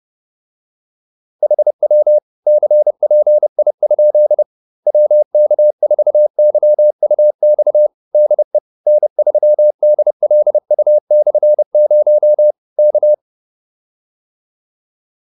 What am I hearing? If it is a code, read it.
Morse code, heard as a keyed tone, about 30 wpm, 600 Hz. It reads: HW CPI? WK4YUX DE N3DLU/0 K